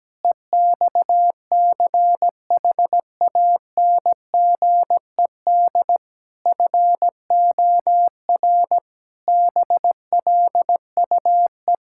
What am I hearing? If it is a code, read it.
Morse code, heard as a keyed tone, about 17 wpm, 700 Hz. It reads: EXCHANGED FOR BLUE
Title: Pangram Challenge v2 - 17wpm